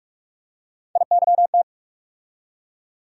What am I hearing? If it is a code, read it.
Morse code, heard as a keyed tone, about 44 wpm, 700 Hz. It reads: IYT
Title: USA Call Sign Postfixes - 3 Letters 45wpm